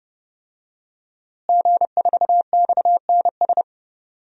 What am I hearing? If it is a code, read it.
Morse code, heard as a keyed tone, about 30 wpm, 700 Hz. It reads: G4XNS
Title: DXCC-100 Call Signs - 5 Characters - Repeated 2x - 30wpm